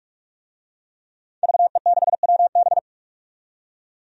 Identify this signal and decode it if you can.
Morse code, heard as a keyed tone, about 44 wpm, 700 Hz. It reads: VE6WB
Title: DXCC-100 Call Signs - 5 Characters - 45wpm